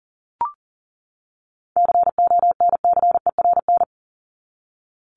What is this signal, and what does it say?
Morse code, heard as a keyed tone, about 40 wpm, 700 Hz. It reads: CONCERN
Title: Top 751-1000 Words 40wpm